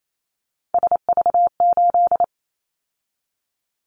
Morse code, heard as a keyed tone, about 28 wpm, 700 Hz. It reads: SV8
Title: Callsign Prefixes for Top 50 DXCC - 28wpm